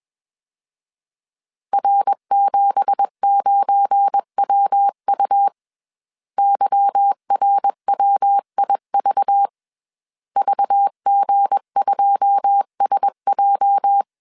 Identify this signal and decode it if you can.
Morse code, heard as a keyed tone, about 21 wpm, 800 Hz. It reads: R79WU YRWIV VG2SJ
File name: CW / DET / 1 9 1500C S C 20wpm 2310